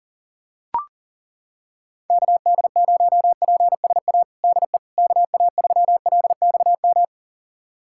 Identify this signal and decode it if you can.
Morse code, heard as a keyed tone, about 40 wpm, 700 Hz. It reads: KD0PSA DE KA3LXM
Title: USA Callsign DE Callsign - 6 Characters - 40wpm